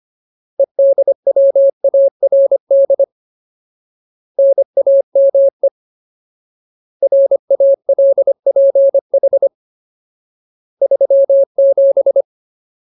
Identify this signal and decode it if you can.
Morse code, heard as a keyed tone, about 25 wpm, 550 Hz. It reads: EDWARD NAME RALPH 37
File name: Ultimate Word List / 25WPM Ultimate